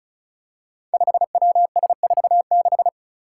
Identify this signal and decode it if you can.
Morse code, heard as a keyed tone, about 36 wpm, 700 Hz. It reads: 5WS46
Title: Five Challenging Letter-Number Combinations 35wpm